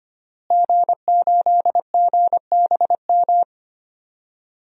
Morse code, heard as a keyed tone, about 25 wpm, 700 Hz. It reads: G8GBM
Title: DXCC-100 Call Signs - 5 Characters - Repeated 2x - 25wpm